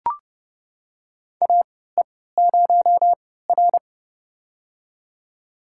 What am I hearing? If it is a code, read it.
Morse code, heard as a keyed tone, about 30 wpm, spaced as for 17 wpm, 700 Hz. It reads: AE0R
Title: USA Call Signs - 4 Characters - 3x Character Spacing - 30wpm